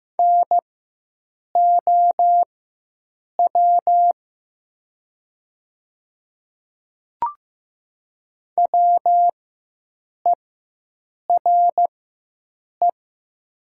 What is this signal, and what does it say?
Morse code, heard as a keyed tone, about 15 wpm, spaced as for 7 wpm, 700 Hz. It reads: NOW WERE
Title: 4x Character Spacing - Top 100 Words - 15wpm